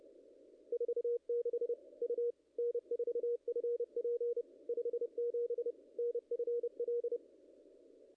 Morse code, heard as a keyed tone, about 28 wpm, 440 Hz. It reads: 46 U N4FP 57 NFL
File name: contests / ss-cw-k5tr-op-n5rz -0000